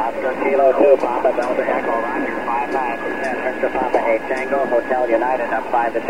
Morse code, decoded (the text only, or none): none